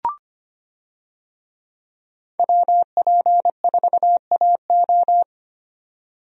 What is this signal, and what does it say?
Morse code, heard as a keyed tone, about 25 wpm, 700 Hz. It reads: WP4AO
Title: USA Call Signs - 5 Characters - 25wpm